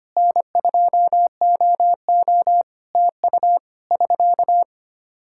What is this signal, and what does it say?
Morse code, heard as a keyed tone, about 25 wpm, 700 Hz. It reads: N2OO TU <SK>
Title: K1USN SST QSO Practice 25wpm